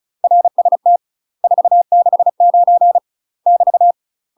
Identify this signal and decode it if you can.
Morse code, heard as a keyed tone, about 36 wpm, 700 Hz. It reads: RST 469 <BT>
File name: UK MASTER QSOs TRAINING / 35WPM / UK MASTER QSOs 0033